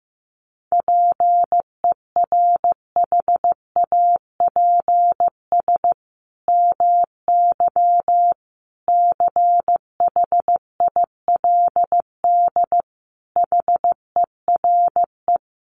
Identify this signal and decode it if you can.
Morse code, heard as a keyed tone, about 15 wpm, 700 Hz. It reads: PERHAPS MY CHILD HERE
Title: Sentences from Top 1100 Words 15wpm